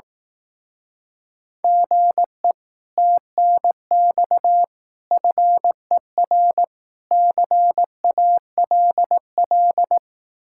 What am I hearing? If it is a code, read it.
Morse code, heard as a keyed tone, about 18 wpm, 700 Hz. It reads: GE TNX FER CALL
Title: Speed-Racing Common QSO Phrases Encore - 15wpm